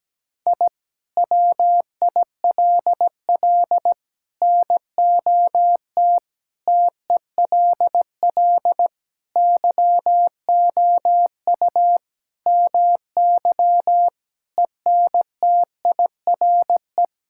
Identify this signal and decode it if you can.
Morse code, heard as a keyed tone, about 17 wpm, 700 Hz. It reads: I WILL NOT TELL YOU MY ENTIRE
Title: Sentences from Top 1000 Words - Review Part 5 - 17wpm